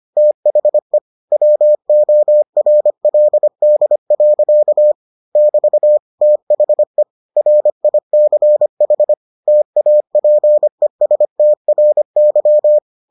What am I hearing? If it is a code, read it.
Morse code, heard as a keyed tone, about 25 wpm, 600 Hz. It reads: THE WORLD. <BT> THE RICH TAPESTRY